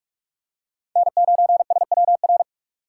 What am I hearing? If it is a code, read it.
Morse code, heard as a keyed tone, about 46 wpm, 700 Hz. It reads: N9SWR
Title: USA Call Signs - 5 Characters - 4x Character Spacing - 45wpm